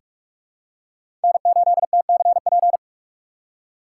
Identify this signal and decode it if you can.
Morse code, heard as a keyed tone, about 42 wpm, 700 Hz. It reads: N8TCP
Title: USA Call Signs - 5 Characters - 2x Character Spacing - 45wpm